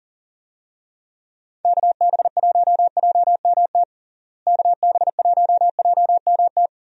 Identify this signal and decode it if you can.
Morse code, heard as a keyed tone, about 40 wpm, 700 Hz. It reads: KB1JMT KB1JMT
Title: USA Call Signs - 6 Characters - Repeated 3x - 40wpm